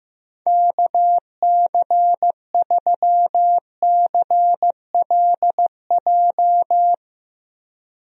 Morse code, heard as a keyed tone, about 15 wpm, 700 Hz. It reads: KC3CLJ
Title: USA Call Signs - 6 Characters - Repeated 3x - 15wpm